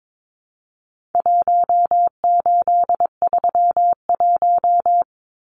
Morse code, heard as a keyed tone, about 22 wpm, 700 Hz. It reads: 1831
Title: Numbers - Contest Style - 4-Digit 22wpm